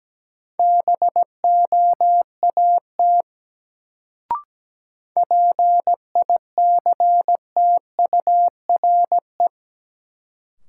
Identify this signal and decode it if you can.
Morse code, heard as a keyed tone, about 17 wpm, 700 Hz. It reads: BOAT PICTURE
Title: Sentences from Top 500 Words - New Word Drill - 17wpm